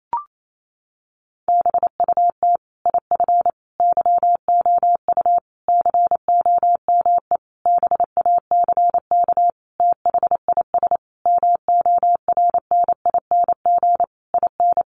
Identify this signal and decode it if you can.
Morse code, heard as a keyed tone, about 28 wpm, 700 Hz. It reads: BUT IF YOU COME BACK THIS MORNING IN
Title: A Sound Of Thunder 28wpm